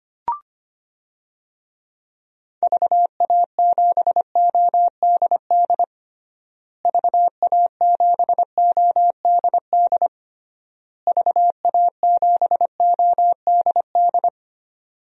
Morse code, heard as a keyed tone, about 25 wpm, 700 Hz. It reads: VA7ODD VA7ODD VA7ODD
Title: DXCC-100 Call Signs - 6 Characters - Repeated 3x - 25wpm